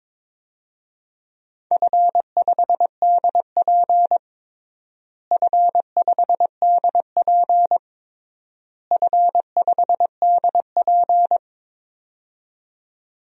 Morse code, heard as a keyed tone, about 22 wpm, 700 Hz. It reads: F5DP F5DP F5DP
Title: DXCC-100 Call Signs - 4 Characters - Repeated 3x - 22wpm